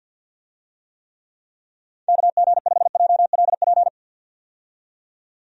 Morse code, heard as a keyed tone, about 49 wpm, 700 Hz. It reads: KG5JLP
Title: USA Call Signs - 6 Characters - 50wpm